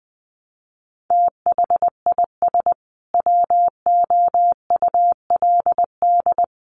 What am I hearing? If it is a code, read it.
Morse code, heard as a keyed tone, about 20 wpm, 700 Hz. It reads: THIS WOULD
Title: Sentences from Top 1200 Words 20wpm